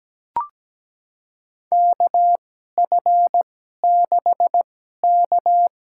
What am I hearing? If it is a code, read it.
Morse code, heard as a keyed tone, about 17 wpm, spaced as for 12 wpm, 700 Hz. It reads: KF6K
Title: USA Call Signs - 2x Character Spacing - 17wpm